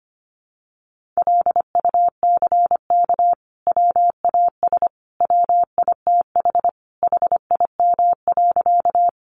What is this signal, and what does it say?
Morse code, heard as a keyed tone, about 25 wpm, 700 Hz. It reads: LUCK WAS WITH HIM.
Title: Sentences from Top 1500 Words 25wpm